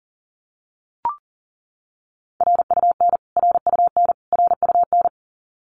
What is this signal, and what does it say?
Morse code, heard as a keyed tone, about 40 wpm, 700 Hz. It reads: RUN RUN RUN
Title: Top 500 Words Repeated 40wpm